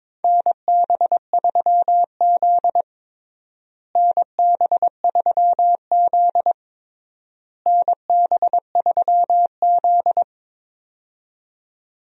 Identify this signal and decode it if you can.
Morse code, heard as a keyed tone, about 22 wpm, 700 Hz. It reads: NB3Z NB3Z NB3Z
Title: USA Call Signs - 4 Characters - Repeated 3x - 22wpm